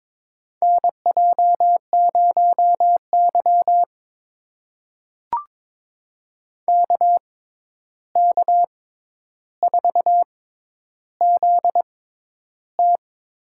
Morse code, heard as a keyed tone, about 22 wpm, spaced as for 11 wpm, 700 Hz. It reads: NJ0Y KK4ZT
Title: USA Call Signs - 6x Character Spacing - 22wpm